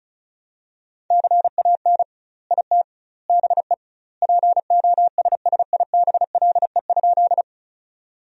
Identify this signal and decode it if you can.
Morse code, heard as a keyed tone, about 36 wpm, 700 Hz. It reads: CAN IT BE POSSIBLE?
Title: 2x Word-Spacing - Sentences from Top 500 Words - 35wpm